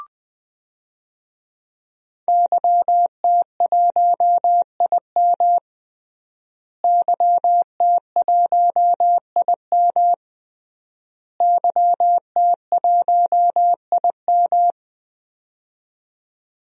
Morse code, heard as a keyed tone, about 21 wpm, 700 Hz. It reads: YT1IM YT1IM YT1IM
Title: DXCC-100 Call Signs - 5 Characters - Repeated 3x - 20wpm